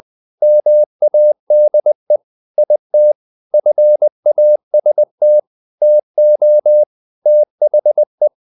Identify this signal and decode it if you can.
Morse code, heard as a keyed tone, about 20 wpm, 600 Hz. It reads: MADE IT FAST TO THE
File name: the old man and the sea part 2 20wpm